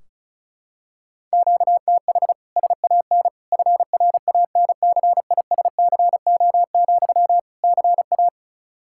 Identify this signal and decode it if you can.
Morse code, heard as a keyed tone, about 34 wpm, 700 Hz. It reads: QTH SAN FRANCISCO, CA
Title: Practice QSO - Exchange 1 - Beginning - 35wpm